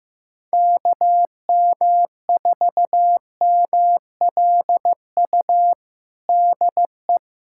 Morse code, heard as a keyed tone, about 15 wpm, 700 Hz. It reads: KM4MLU DE
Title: USA Callsign DE Callsign 15wpm